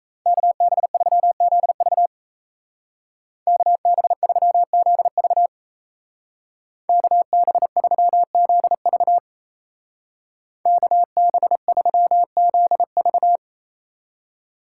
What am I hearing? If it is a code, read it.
Morse code, heard as a keyed tone, about 34 wpm, 700 Hz. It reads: KB3ZV KB3ZV KB3ZV KB3ZV
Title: Speed Racing - USA 2x2 Call Signs 28wpm